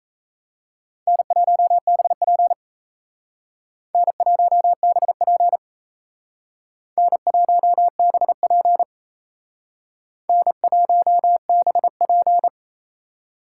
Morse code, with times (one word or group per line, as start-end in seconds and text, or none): 1.1-2.5 N1BP
3.9-5.6 N1BP
7.0-8.8 N1BP
10.3-12.5 N1BP